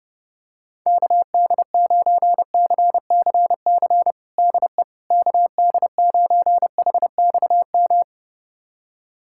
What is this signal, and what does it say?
Morse code, heard as a keyed tone, about 30 wpm, 700 Hz. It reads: KD9CCC DE KD9HXM
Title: USA Callsign DE Callsign - 6 Characters - 30wpm